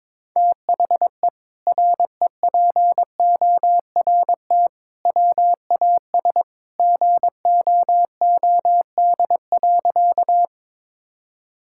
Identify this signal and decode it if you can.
Morse code, heard as a keyed tone, about 22 wpm, 700 Hz. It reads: THE REPORT WAS GOOD.
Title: Sentences from Top 1000 Words - Review Part 6 - 22wpm